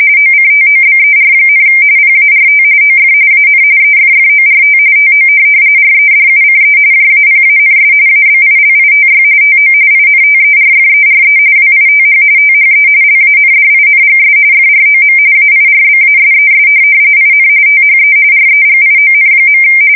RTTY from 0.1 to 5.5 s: OCTOBER, PART OF AN EFFORT TO